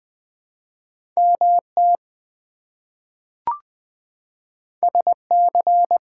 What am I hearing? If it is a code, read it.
Morse code, heard as a keyed tone, about 21 wpm, 700 Hz. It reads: MT SC
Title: ARRL Sections v2 - 20wpm